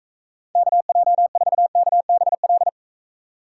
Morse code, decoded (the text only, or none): KJ4KBL